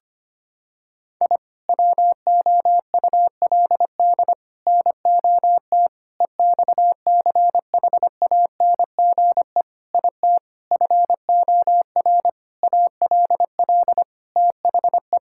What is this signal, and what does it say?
Morse code, heard as a keyed tone, about 25 wpm, 700 Hz. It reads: I WOULD NOT EXCHANGE IT FOR ALL THE
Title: Sentences from Top 1000 Words - Review Part 6 - 25wpm